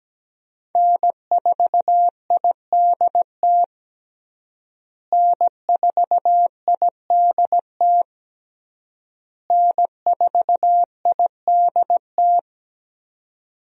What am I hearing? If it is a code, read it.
Morse code, heard as a keyed tone, about 17 wpm, 700 Hz. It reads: N4IDT N4IDT N4IDT
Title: USA Call Signs - 5 Characters - Repeated 3x - 17wpm